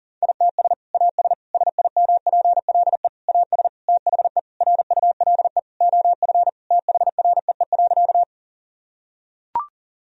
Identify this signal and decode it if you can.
Morse code, heard as a keyed tone, about 40 wpm, 700 Hz. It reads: ITS AS SIMPLE AS THE RULE OF THREE.
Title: Sentences from Top 1000 Words - Review Part 2 - 40wpm